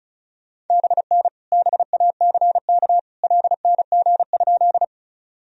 Morse code, heard as a keyed tone, about 36 wpm, 700 Hz. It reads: BN BACK LNG?
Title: Sentences from Top 200 Words and Abbreviations 35wpm